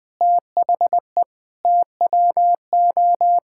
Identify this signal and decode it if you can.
Morse code, heard as a keyed tone, about 20 wpm, 700 Hz. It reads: THE TWO
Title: Sentences from Top 1000 Words - Review Part 5 - 20wpm